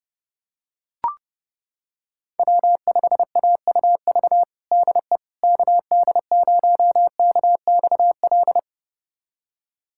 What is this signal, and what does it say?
Morse code, heard as a keyed tone, about 32 wpm, 700 Hz. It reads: W5AUV DE KD0KXL
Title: USA Callsign DE Callsign 30wpm